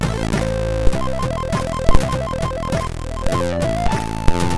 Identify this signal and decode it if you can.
There is no code.